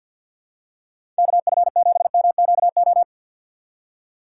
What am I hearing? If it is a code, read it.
Morse code, heard as a keyed tone, about 47 wpm, 700 Hz. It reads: KF7MQO
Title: USA Call Signs - 6 Characters - 6x Character Spacing - 50wpm